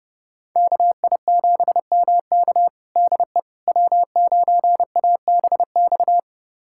Morse code, heard as a keyed tone, about 32 wpm, 700 Hz. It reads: KI7MK DE W9ABX
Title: USA Callsign DE Callsign - 5 Characters - 30wpm